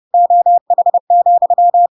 Morse code, heard as a keyed tone, about 30 wpm, 700 Hz. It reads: OH,
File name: Pinocchio / Pinocchio ch7 30 30 ik1hge